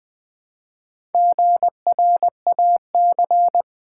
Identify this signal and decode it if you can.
Morse code, heard as a keyed tone, about 20 wpm, 700 Hz. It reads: GRAC
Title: Build A Word - 1 Letter At A Time - 3 to 5-Letter Words - 20wpm